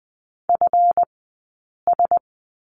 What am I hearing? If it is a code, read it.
Morse code, heard as a keyed tone, about 20 wpm, 700 Hz. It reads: F S
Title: 204 Lesson 33d - Letter j - Two Characters 20wpm